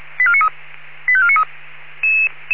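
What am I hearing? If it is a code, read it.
Five-tone selective call (ZVEI), heard as quick runs of five tones, tones of about 70 ms. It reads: 85493
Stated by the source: ZVEI five-tone selective call received by SDR.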